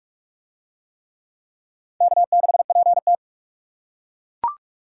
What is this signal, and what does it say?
Morse code, heard as a keyed tone, about 41 wpm, 700 Hz. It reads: K6PT